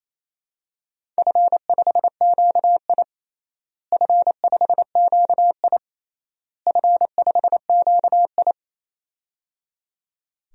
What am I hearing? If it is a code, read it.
Morse code, heard as a keyed tone, about 28 wpm, 700 Hz. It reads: F5QI F5QI F5QI
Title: DXCC-100 Call Signs - 4 Characters - Repeated 3x - 28wpm